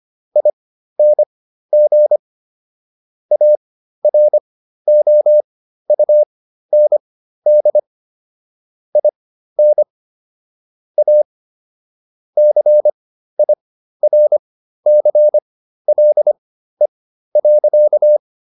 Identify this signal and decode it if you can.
Morse code, heard as a keyed tone, about 25 wpm, spaced as for 13 wpm, 600 Hz. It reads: ING AROUND IN A CIRCLE.